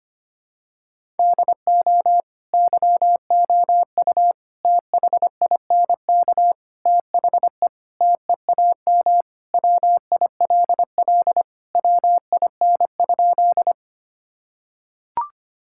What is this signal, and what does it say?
Morse code, heard as a keyed tone, about 25 wpm, 700 Hz. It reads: DO YOU THINK THE TEAM WILL WIN?
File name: practice / 25wpm / Sentences with 4-Word Phrases - 25wpm